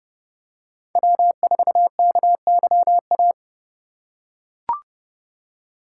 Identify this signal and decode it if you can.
Morse code, heard as a keyed tone, about 30 wpm, 700 Hz. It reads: W4KYA
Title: USA Call Signs - 5 Characters - 4x Character Spacing - 30wpm